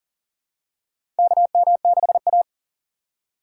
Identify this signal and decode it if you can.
Morse code, heard as a keyed tone, about 37 wpm, 700 Hz. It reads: KM6A
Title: USA Call Signs - 4 Characters - 4x Character Spacing - 40wpm